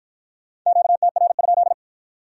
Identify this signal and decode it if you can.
Morse code, heard as a keyed tone, about 53 wpm, 700 Hz. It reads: QTR?